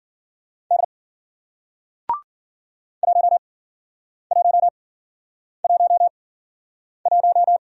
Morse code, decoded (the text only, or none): D 1111